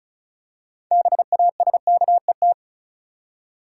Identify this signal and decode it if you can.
Morse code, heard as a keyed tone, about 36 wpm, 700 Hz. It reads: BASKET
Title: Six Letter Words - 35wpm